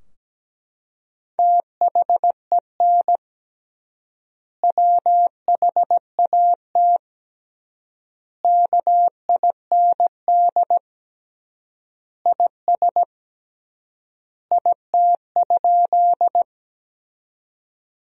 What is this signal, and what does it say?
Morse code, heard as a keyed tone, about 17 wpm, 700 Hz. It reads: THEN WHAT KIND IS IT?
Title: 3x Word-Spacing - Sentences from Top 200 Words - 17wpm